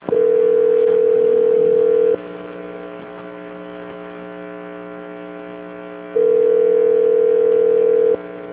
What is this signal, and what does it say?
Ringback tone (North America), 440 + 480 Hz.
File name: out- - - - - .20809